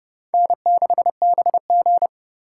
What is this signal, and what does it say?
Morse code, heard as a keyed tone, about 28 wpm, 700 Hz. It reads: N6BG